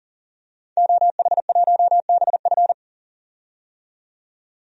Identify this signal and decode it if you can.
Morse code, heard as a keyed tone, about 39 wpm, 700 Hz. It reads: OH1BF
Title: DXCC-100 Call Signs - 5 Characters - 40wpm